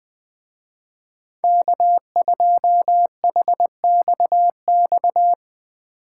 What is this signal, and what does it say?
Morse code, heard as a keyed tone, about 21 wpm, 700 Hz. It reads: K2HXX
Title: Speed Racing - USA 1x3 Call Signs 20wpm